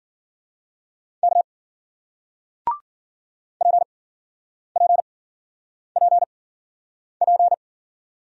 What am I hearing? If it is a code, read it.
Morse code, heard as a keyed tone, about 53 wpm, spaced as for 9 wpm, 700 Hz. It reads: K PPPP